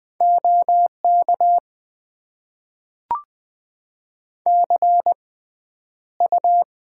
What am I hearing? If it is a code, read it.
Morse code, heard as a keyed tone, about 20 wpm, spaced as for 10 wpm, 700 Hz. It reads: OK CU
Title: Top 90 QSO Elements - 6x Character Spacing - 20wpm